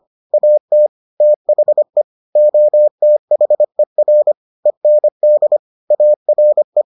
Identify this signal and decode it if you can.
Morse code, heard as a keyed tone, about 25 wpm, 600 Hz. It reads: AT THE OTHER END ARE